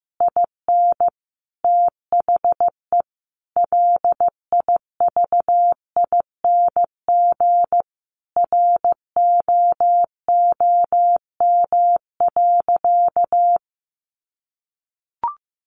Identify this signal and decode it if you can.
Morse code, heard as a keyed tone, about 14 wpm, 700 Hz. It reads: IN THE LIVING ROOM.
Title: Sentences from Top 1200 Words 15wpm